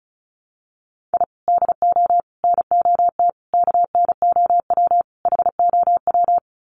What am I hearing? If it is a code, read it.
Morse code, heard as a keyed tone, about 34 wpm, 700 Hz. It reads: I DO NOT KNOW HOW
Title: Sets of 5 Words 35wpm